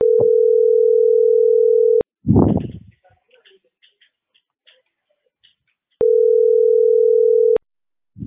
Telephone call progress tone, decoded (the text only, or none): ringback tone